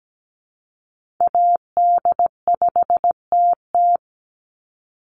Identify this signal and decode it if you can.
Morse code, heard as a keyed tone, about 17 wpm, 700 Hz. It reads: AD5TT
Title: Call Signs - 17wpm